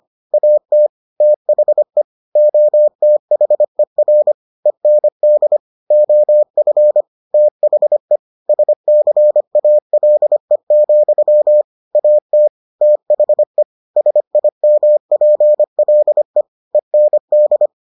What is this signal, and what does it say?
Morse code, heard as a keyed tone, about 25 wpm, 600 Hz. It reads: AT THE OTHER END OF THE SCALE, AT THE SIMPLE END